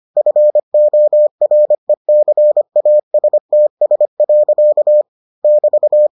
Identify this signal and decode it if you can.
Morse code, heard as a keyed tone, about 25 wpm, 600 Hz. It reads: FORECASTS. <BT>